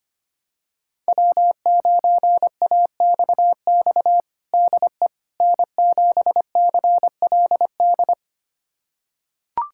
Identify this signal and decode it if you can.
Morse code, heard as a keyed tone, about 25 wpm, 700 Hz. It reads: W9AXX DE N7CLD
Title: USA Callsign DE Callsign - 5 Characters - 25wpm